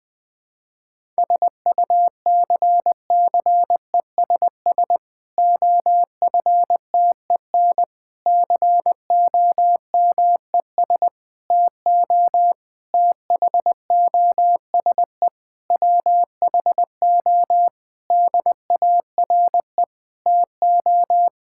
Morse code, heard as a keyed tone, about 20 wpm, 700 Hz. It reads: SUCCESS OFTEN COMES TO THOSE WHO DARE TO
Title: Difficult Sentences with Hints - 20wpm